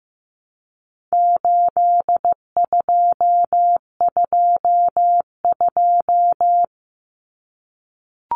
Morse code, heard as a keyed tone, about 15 wpm, 700 Hz. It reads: 8222
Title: Common Cut Numbers - 4-Digit 15wpm